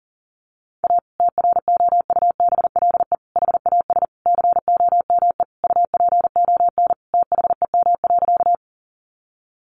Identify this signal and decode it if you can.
Morse code, heard as a keyed tone, about 40 wpm, 700 Hz. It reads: A TROUBLE HAS COME UPON THEM.
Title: Sentences from Top 1200 Words 40wpm